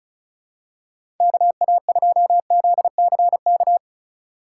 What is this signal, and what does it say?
Morse code, heard as a keyed tone, about 36 wpm, 700 Hz. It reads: KA2ZCK
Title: USA Call Signs - 6 Characters - 2x Character Spacing - 35wpm